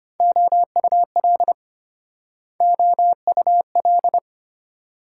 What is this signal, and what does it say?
Morse code, heard as a keyed tone, about 32 wpm, 700 Hz. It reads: OUL OUL